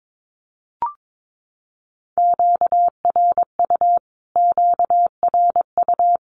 Morse code, heard as a keyed tone, about 22 wpm, 700 Hz. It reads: QRU QRU
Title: Q-Codes Repeated 3x 22wpm